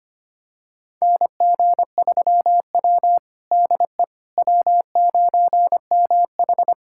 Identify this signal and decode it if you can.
Morse code, heard as a keyed tone, about 25 wpm, 700 Hz. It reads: NG3W DE W9MH
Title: USA Callsign DE Callsign - 4 Characters - 25wpm